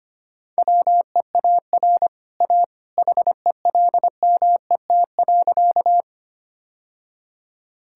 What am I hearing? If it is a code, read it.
Morse code, heard as a keyed tone, about 25 wpm, 690 Hz. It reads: WEAR A HELMET.
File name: Easy Sentences with Hints - 25wpm